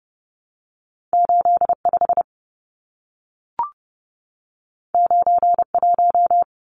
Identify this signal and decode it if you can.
Morse code, heard as a keyed tone, about 30 wpm, 700 Hz. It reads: 85 91
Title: Numbers - 2-Digit 30wpm